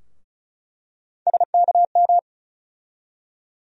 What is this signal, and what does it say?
Morse code, heard as a keyed tone, about 36 wpm, 700 Hz. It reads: SKM
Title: USA Call Sign Postfixes - 3 Letters 35wpm